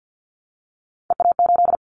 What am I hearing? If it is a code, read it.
Morse code, heard as a keyed tone, about 50 wpm, 700 Hz. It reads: EA8
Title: Callsign Prefixes for Top 50 DXCC - 50wpm